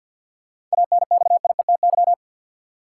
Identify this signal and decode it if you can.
Morse code, heard as a keyed tone, about 51 wpm, 700 Hz. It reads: ANXIETY
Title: Sentences from Top 1600 Words - New Word Drill Alt - 50wpm